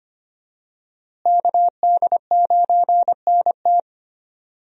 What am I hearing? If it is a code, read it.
Morse code, heard as a keyed tone, about 25 wpm, 700 Hz. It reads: KD9NT
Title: USA Call Signs - 5 Characters - Repeated 2x - 25wpm